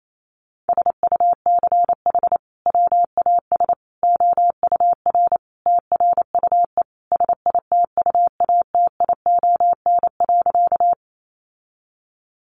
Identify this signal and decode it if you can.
Morse code, heard as a keyed tone, about 28 wpm, 700 Hz. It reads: SUCH WAS OUR TRUE SITUATION.
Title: Sentences from Top 1200 Words 28wpm